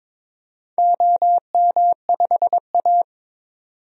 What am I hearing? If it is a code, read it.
Morse code, heard as a keyed tone, about 22 wpm, 700 Hz. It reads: OM5A
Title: DXCC-100 Call Signs - 4 Characters - 22wpm